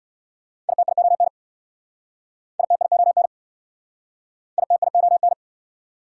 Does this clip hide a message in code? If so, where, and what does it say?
Yes: Morse code from 0.7 to 5.3 s, heard as a keyed tone, about 69 wpm, 690 Hz. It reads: ITION ITION ITION